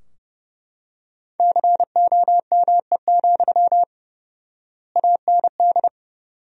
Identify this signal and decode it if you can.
Morse code, heard as a keyed tone, about 32 wpm, 700 Hz. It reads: COME, AND
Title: 4x Word-Spacing - Sentences from Top 500 Words - 30wpm